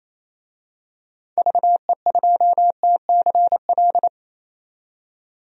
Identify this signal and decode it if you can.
Morse code, heard as a keyed tone, about 28 wpm, 700 Hz. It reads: VE2TCL
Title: DXCC-100 Call Signs - 6 Characters - 28wpm